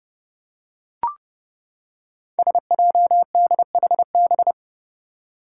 Morse code, heard as a keyed tone, about 30 wpm, 700 Hz. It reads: SJDHB
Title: Five Challenging Letter-Number Combinations 30wpm